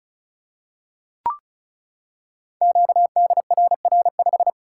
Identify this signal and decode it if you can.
Morse code, heard as a keyed tone, about 36 wpm, 690 Hz. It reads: QDRR5
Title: Code Groups - 5 Letter:Numbers - 35wpm